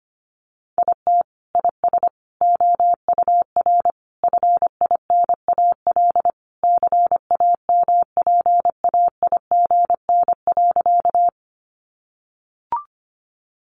Morse code, heard as a keyed tone, about 25 wpm, 700 Hz. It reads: IT IS OUR FINAL CAMPAIGN.
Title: Sentences from Top 1200 Words 25wpm